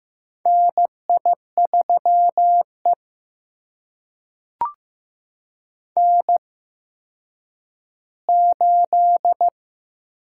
Morse code, heard as a keyed tone, about 15 wpm, spaced as for 9 wpm, 700 Hz. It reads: NI3E N8
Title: USA Call Signs - 4 Characters - 8x Character Spacing - 15wpm